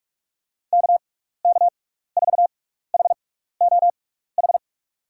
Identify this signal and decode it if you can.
Morse code, heard as a keyed tone, about 47 wpm, spaced as for 16 wpm, 700 Hz. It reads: KK4HOH
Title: USA Call Signs - 6 Characters - 6x Character Spacing - 45wpm